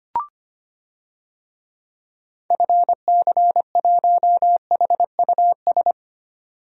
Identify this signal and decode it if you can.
Morse code, heard as a keyed tone, about 25 wpm, 700 Hz. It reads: FC1HUS
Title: DXCC-100 Call Signs - 6 Characters - 25wpm